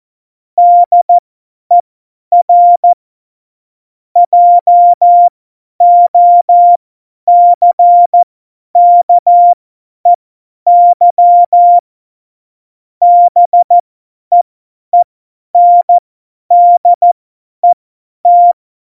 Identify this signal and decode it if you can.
Morse code, heard as a keyed tone, about 14 wpm, spaced as for 10 wpm, 700 Hz. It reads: DER JOCKEY BEENDET